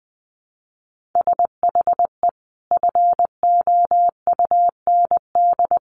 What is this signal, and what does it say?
Morse code, heard as a keyed tone, about 19 wpm, 700 Hz. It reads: SHE FOUND
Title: Sentences from Top 1500 Words 20wpm